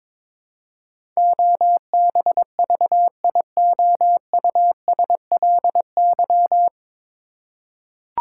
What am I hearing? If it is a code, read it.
Morse code, heard as a keyed tone, about 22 wpm, 700 Hz. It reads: OBVIOUSLY
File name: 9 Letter Words - First 3 Letters Given - 22wpm